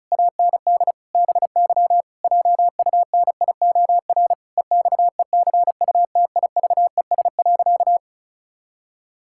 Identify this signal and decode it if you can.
Morse code, heard as a keyed tone, about 34 wpm, 700 Hz. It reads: AND BY JUNIOR EXECUTIVES.
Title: Pangram Challenge v2 - 35wpm